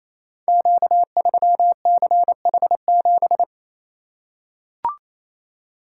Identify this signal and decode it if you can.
Morse code, heard as a keyed tone, about 29 wpm, 700 Hz. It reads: Q3CH7
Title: Code Groups - 5 Letter:Numbers - 28wpm